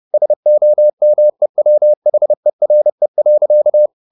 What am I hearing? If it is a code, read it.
Morse code, heard as a keyed tone, about 30 wpm, 600 Hz. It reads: SOMEWHERE.
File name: the old man and the sea / the old man and the sea 1 30wpm